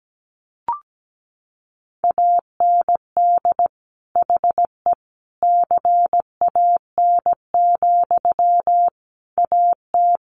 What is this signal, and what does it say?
Morse code, heard as a keyed tone, about 17 wpm, 700 Hz. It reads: AND HE CAN, AT
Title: Sentences from Top 900 Words 17wpm